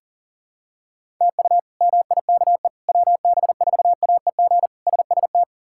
Morse code, heard as a keyed tone, about 40 wpm, 700 Hz. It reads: TU MIKE WB4AEG SST
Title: K1USN SST QSO Practice - 40wpm